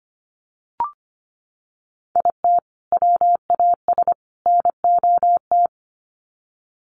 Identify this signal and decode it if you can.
Morse code, heard as a keyed tone, about 25 wpm, 700 Hz. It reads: IT WAS NOT
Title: Sets of 3 Words 25wpm